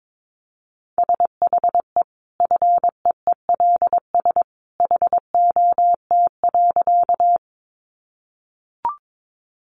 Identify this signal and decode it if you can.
Morse code, heard as a keyed tone, about 22 wpm, 700 Hz. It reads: SHE FEELS HOT.
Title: Additional Easy Phrases 22wpm